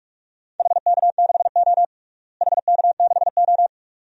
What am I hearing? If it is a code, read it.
Morse code, heard as a keyed tone, about 41 wpm, 700 Hz. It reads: HK6O HK6O
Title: DXCC-100 Call Signs - 4 Characters - Repeated 2x - 45wpm